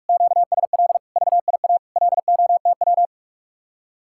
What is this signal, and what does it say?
Morse code, heard as a keyed tone, about 49 wpm, 700 Hz. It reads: QSL VIA LOTW